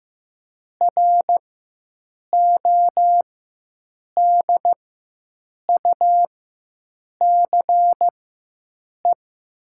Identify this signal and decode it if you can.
Morse code, heard as a keyed tone, about 15 wpm, spaced as for 7 wpm, 700 Hz. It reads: RODUCE
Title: 4x Character Spacing - Top 500 Words - 15wpm